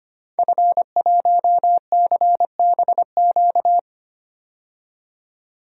Morse code, heard as a keyed tone, about 25 wpm, 700 Hz. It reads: F1CBQ
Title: DXCC-100 Call Signs - 5 Characters - 25wpm